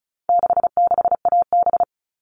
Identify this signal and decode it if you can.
Morse code, heard as a keyed tone, about 34 wpm, 700 Hz. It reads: 66AB
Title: All Cut Numbers - 4-Digit 35wpm